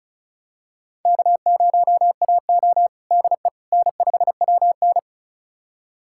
Morse code, heard as a keyed tone, about 34 wpm, 700 Hz. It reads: K0AO DE N5WN
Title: USA Callsign DE Callsign - 4 Characters - 35wpm